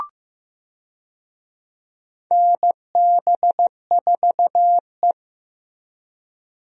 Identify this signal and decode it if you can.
Morse code, heard as a keyed tone, about 16 wpm, 700 Hz. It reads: NB4E
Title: USA Call Signs - 4 Characters - Repeated 2x - 15wpm